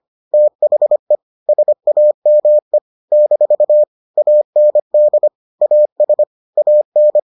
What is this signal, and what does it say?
Morse code, heard as a keyed tone, about 25 wpm, 600 Hz. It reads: THE SAME - AND AS AN